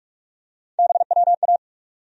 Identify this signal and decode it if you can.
Morse code, heard as a keyed tone, about 47 wpm, 700 Hz. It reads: BWA
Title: USA Call Sign Postfixes - 3 Letters 45wpm